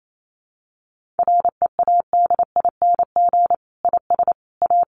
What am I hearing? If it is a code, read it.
Morse code, heard as a keyed tone, about 28 wpm, 700 Hz. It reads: READING IS A
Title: Sentences from Top 1400 Words 28wpm